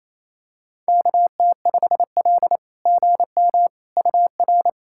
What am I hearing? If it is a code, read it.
Morse code, heard as a keyed tone, about 28 wpm, 700 Hz. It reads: KT5L GM UR
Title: SOTA QSO Practice 28wpm